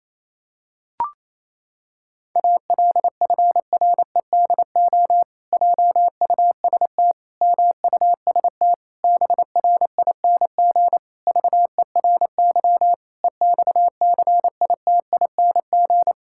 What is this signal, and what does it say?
Morse code, heard as a keyed tone, about 28 wpm, 700 Hz. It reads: ALFREDO JUST MUST BRING VERY EXCITING
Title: Pangram Challenge v2 - 28wpm